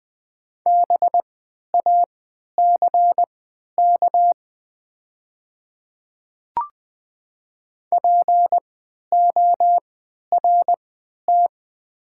Morse code, heard as a keyed tone, about 20 wpm, spaced as for 11 wpm, 700 Hz. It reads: BACK PORT
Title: 3x Character Spacing - Top 200 Words - 20wpm